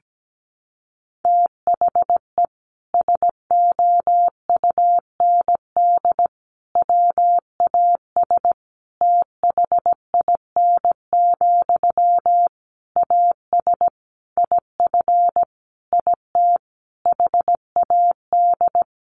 Morse code, heard as a keyed tone, about 17 wpm, 700 Hz. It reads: THE SOUND WAS THIN, AS IF IT HAD
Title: Sentences from Top 1200 Words 17wpm